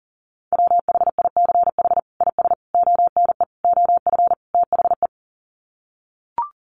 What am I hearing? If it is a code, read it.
Morse code, heard as a keyed tone, about 40 wpm, 700 Hz. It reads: WHICH IS ONE OF THE
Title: Sets of 5 Words Encore 40wpm